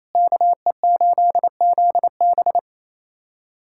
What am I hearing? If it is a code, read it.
Morse code, heard as a keyed tone, about 28 wpm, 700 Hz. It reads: KE8ZB
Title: USA Callsign DE Callsign - 5 Characters - 28wpm